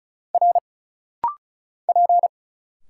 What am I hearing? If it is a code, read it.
Morse code, heard as a keyed tone, about 34 wpm, 700 Hz. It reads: R P
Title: Single Letters - Rapid-Fire 35wpm